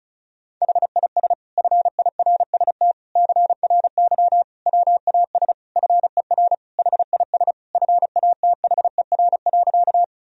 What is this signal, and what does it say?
Morse code, heard as a keyed tone, about 34 wpm, 700 Hz. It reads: HIS FIRST CRY WAS FER HIS FATHER.
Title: Sentences from Top 500 Words and Abbreviations 35wpm